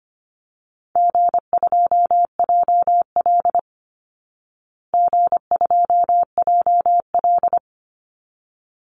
Morse code, heard as a keyed tone, about 25 wpm, 700 Hz. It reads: G2JL G2JL
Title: Easy Call Signs Repeated 2x - 25wpm